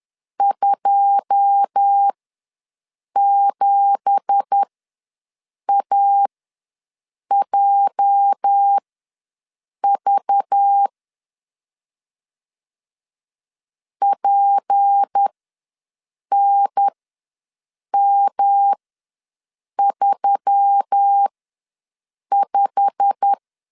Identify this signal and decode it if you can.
Morse code, heard as a keyed tone, about 10 wpm, spaced as for 6 wpm, 800 Hz. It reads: 27AJV PNM35
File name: LEZIONI / 9 2867 ws 10wpm 4215